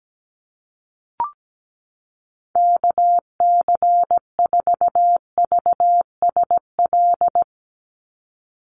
Morse code, heard as a keyed tone, about 17 wpm, 700 Hz. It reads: KC4VSL
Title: USA Call Signs 17wpm